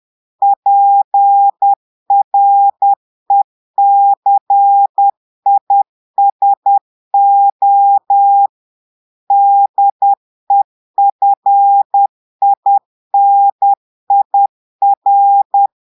Morse code, heard as a keyed tone, about 10 wpm, 800 Hz. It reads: PRECISO DEFINIR
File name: portugais texte en clair 2 vitesse 10 mots minutes